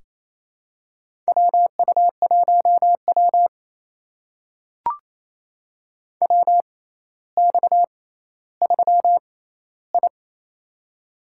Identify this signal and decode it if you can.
Morse code, heard as a keyed tone, about 28 wpm, spaced as for 14 wpm, 700 Hz. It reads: WU1W WX3I
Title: USA Call Signs - 4 Characters - 6x Character Spacing - 28wpm